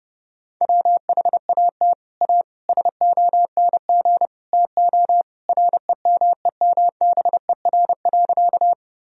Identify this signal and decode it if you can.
Morse code, heard as a keyed tone, about 29 wpm, 700 Hz. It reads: WHAT A SONG TO REMEMBER.
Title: Sentences from Top 1000 Words - Review Part 2 - 30wpm